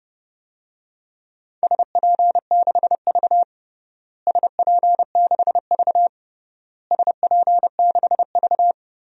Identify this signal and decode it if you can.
Morse code, heard as a keyed tone, about 30 wpm, 700 Hz. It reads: SP6V SP6V SP6V
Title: DXCC-100 Call Signs - 4 Characters - Repeated 3x - 30wpm